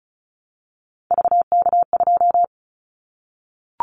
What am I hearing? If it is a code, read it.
Morse code, heard as a keyed tone, about 35 wpm, 700 Hz. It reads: VK2
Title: Callsign Prefixes for Top 50 DXCC - 35wpm